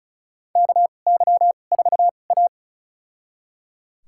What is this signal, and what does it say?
Morse code, heard as a keyed tone, about 36 wpm, spaced as for 25 wpm, 700 Hz. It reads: KY4A
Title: USA Call Signs - 4 Characters - 2x Character Spacing - 35wpm